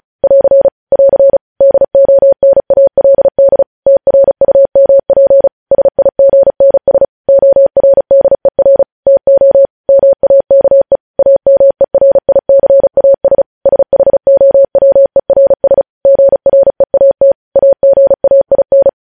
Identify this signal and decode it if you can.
Morse code, heard as a keyed tone, about 36 wpm, 550 Hz. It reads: <AR> <AR> DONALD TRUMP SIGNS ORDER TO MAKE AMERICAS SHOWERS GREAT AGAIN
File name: podcast / News headlines-1x-35x35